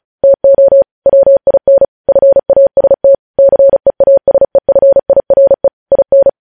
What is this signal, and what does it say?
Morse code, heard as a keyed tone, about 36 wpm, 550 Hz. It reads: TO WIN FAST CEASEFIRE IN
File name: podcast / News headlines-1x-35x35